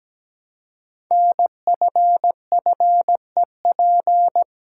Morse code, heard as a keyed tone, about 17 wpm, 700 Hz. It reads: NFFEP
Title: Code Groups - 5 Letters - 17wpm